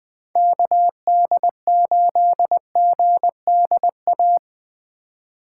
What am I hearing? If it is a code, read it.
Morse code, heard as a keyed tone, about 24 wpm, 700 Hz. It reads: KD8GDA
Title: Speed Racing - USA Call Signs 20wpm